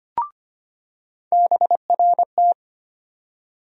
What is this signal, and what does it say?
Morse code, heard as a keyed tone, about 25 wpm, 700 Hz. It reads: BRT
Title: USA Call Sign Postfixes - 3 Letters 25wpm